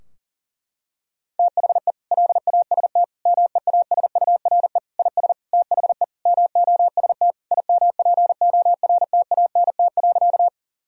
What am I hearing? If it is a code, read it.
Morse code, heard as a keyed tone, about 40 wpm, 700 Hz. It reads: THE LAST MEASURE IS THE MOST IMPORTANT.
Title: Sentences from Top 1000 Words - Review Part 2 - 40wpm